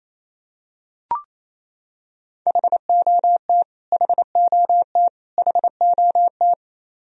Morse code, heard as a keyed tone, about 28 wpm, 700 Hz. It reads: HOT HOT HOT
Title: Top 100 Words Repeated 28wpm